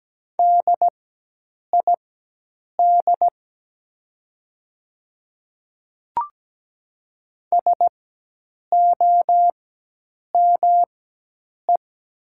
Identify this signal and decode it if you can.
Morse code, heard as a keyed tone, about 17 wpm, spaced as for 8 wpm, 700 Hz. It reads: DID SOME
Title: 4x Character Spacing - Top 100 Words - 17wpm